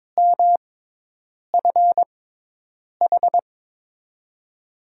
Morse code, heard as a keyed tone, about 22 wpm, spaced as for 8 wpm, 700 Hz. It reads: MFH
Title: USA Call Signs - 6 Characters - 6x Character Spacing - 22wpm